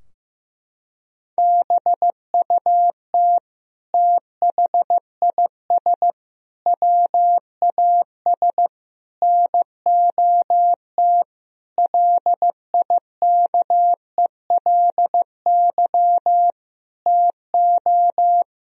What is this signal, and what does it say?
Morse code, heard as a keyed tone, about 15 wpm, 700 Hz. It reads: BUT THIS WAS NOT LIKELY TO
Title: Sentences from Top 1000 Words - Review Part 4 - 15wpm